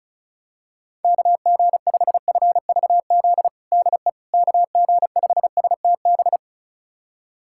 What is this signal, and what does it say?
Morse code, heard as a keyed tone, about 36 wpm, 700 Hz. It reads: KG5FVZ DE KG5STB
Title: USA Callsign DE Callsign - 6 Characters - 35wpm